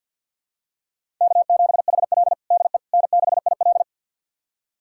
Morse code, heard as a keyed tone, about 49 wpm, 700 Hz. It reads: K7HL DE N6IL
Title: USA Callsign DE Callsign - 4 Characters - 50wpm